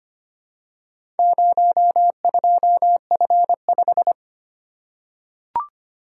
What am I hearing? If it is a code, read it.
Morse code, heard as a keyed tone, about 25 wpm, 700 Hz. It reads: 02F5
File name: practice / 25wpm / Four Letter-Number Combinations - 25wpm